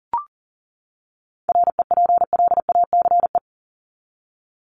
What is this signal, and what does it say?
Morse code, heard as a keyed tone, about 40 wpm, 700 Hz. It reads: REPLACE
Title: Top 1251-1500 Words 40wpm